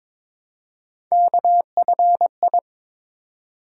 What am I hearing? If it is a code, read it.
Morse code, heard as a keyed tone, about 23 wpm, 700 Hz. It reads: KFI
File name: practice / 22wpm / Three Letter-Number Combinations - 22wpm